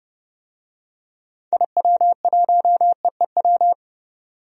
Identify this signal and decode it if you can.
Morse code, heard as a keyed tone, about 30 wpm, 700 Hz. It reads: IW1EEW
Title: DXCC-100 Call Signs - 6 Characters - 30wpm